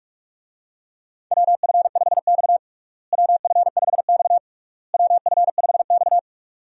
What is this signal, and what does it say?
Morse code, heard as a keyed tone, about 46 wpm, 700 Hz. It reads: WF5X WF5X WF5X
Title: USA Call Signs - 4 Characters - Repeated 3x - 45wpm